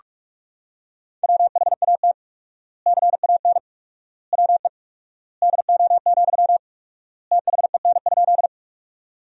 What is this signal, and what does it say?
Morse code, heard as a keyed tone, about 42 wpm, 700 Hz. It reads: WHAT CAN WE DO, THEN?
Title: 4x Word-Spacing - Sentences from Top 100 Words - 45wpm